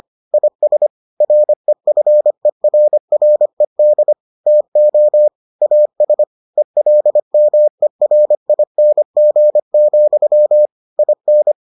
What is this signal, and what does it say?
Morse code, heard as a keyed tone, about 25 wpm, 600 Hz. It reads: IS REFERRED TO AS ELMERING, IN